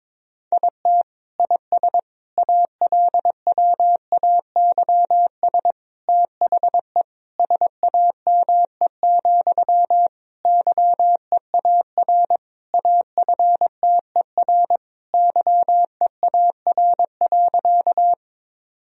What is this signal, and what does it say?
Morse code, heard as a keyed tone, about 22 wpm, 700 Hz. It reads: IT IS ALWAYS THE SAME, YEAR AFTER YEAR.
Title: Sentences from Top 1000 Words - Review Part 1 - 22wpm